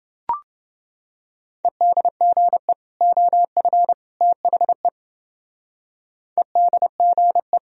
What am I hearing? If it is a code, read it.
Morse code, heard as a keyed tone, about 30 wpm, 700 Hz. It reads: EDGE OF THE EDGE